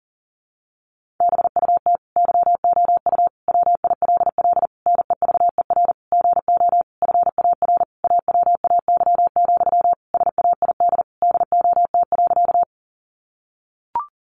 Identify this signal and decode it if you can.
Morse code, heard as a keyed tone, about 40 wpm, 700 Hz. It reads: BUT YOU WILL NEVER GO FAR AWAY, SAID DOT.
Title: Sentences from Top 1500 Words 40wpm